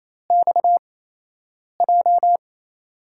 Morse code, heard as a keyed tone, about 28 wpm, spaced as for 8 wpm, 700 Hz. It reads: XJ
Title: USA Call Signs - 5 Characters - 8x Character Spacing - 28wpm